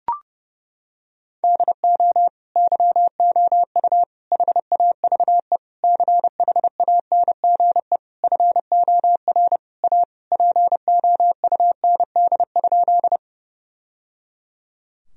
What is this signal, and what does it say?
Morse code, heard as a keyed tone, about 30 wpm, 700 Hz. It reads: DO YOU HAVE CHANGE FOR A POUND?
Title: Sentences from Top 1000 Words - Review Part 2 - 30wpm